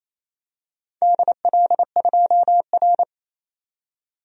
Morse code, heard as a keyed tone, about 28 wpm, 700 Hz. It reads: DL2R
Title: DXCC-100 Call Signs - 4 Characters - 28wpm